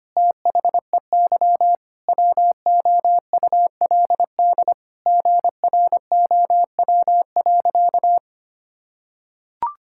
Morse code, heard as a keyed tone, about 25 wpm, 700 Hz. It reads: THEY WOULD GROW.
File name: practice / 25wpm / Sentences with 9-Word Phrases - 25wpm